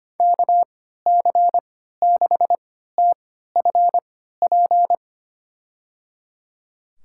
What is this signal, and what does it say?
Morse code, heard as a keyed tone, about 25 wpm, spaced as for 14 wpm, 700 Hz. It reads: KC6TFP
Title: USA Call Signs - 6 Characters - 3x Character Spacing - 25wpm